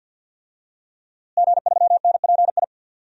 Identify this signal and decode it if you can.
Morse code, heard as a keyed tone, about 48 wpm, 700 Hz. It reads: G3NPI
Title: DXCC-100 Call Signs - 5 Characters - Repeated 2x - 50wpm